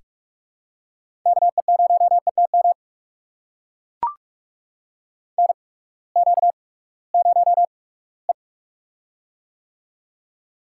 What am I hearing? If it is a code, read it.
Morse code, heard as a keyed tone, about 44 wpm, spaced as for 22 wpm, 700 Hz. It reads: KE0ETM NQ0E
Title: USA Call Signs - 8x Character Spacing - 45wpm